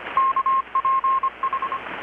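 Morse code, heard as a keyed tone, about 25 wpm, 1070 Hz. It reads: KPH
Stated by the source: Morse from KPH's Night of Nights XX maritime radiotelegraphy event, received in Chiapas, Mexico about 3,800 km away on a Kenwood R-600 with a grounded 10 m long-wire antenna.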